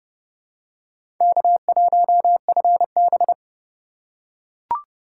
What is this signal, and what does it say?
Morse code, heard as a keyed tone, about 32 wpm, 700 Hz. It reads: K1FB
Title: USA Call Signs - 4 Characters - 3x Character Spacing - 30wpm